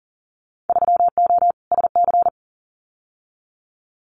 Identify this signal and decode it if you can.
Morse code, heard as a keyed tone, about 38 wpm, 700 Hz. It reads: 3O SC
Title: Winter Field Day Exchanges 40wpm